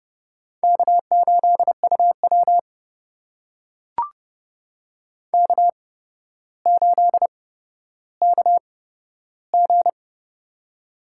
Morse code, heard as a keyed tone, about 30 wpm, spaced as for 13 wpm, 700 Hz. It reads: K8UW K8KG
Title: USA Call Signs - 4 Characters - 8x Character Spacing - 30wpm